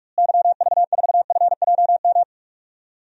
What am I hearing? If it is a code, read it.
Morse code, heard as a keyed tone, about 46 wpm, 700 Hz. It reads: YV4FJM
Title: DXCC-100 Call Signs - 6 Characters - Repeated 2x - 45wpm